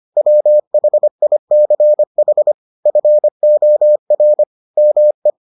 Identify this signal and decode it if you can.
Morse code, heard as a keyed tone, about 25 wpm, 600 Hz. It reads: WHICH FOR ME